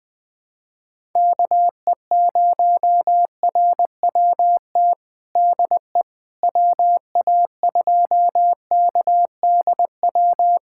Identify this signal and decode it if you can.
Morse code, heard as a keyed tone, about 20 wpm, 700 Hz. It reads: KE0RWT DE WA2KDW
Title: USA Callsign DE Callsign - 6 Characters - 20wpm